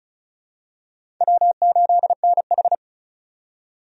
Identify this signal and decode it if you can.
Morse code, heard as a keyed tone, about 36 wpm, 700 Hz. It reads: W8NH
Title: USA Call Signs - 4 Characters - 35wpm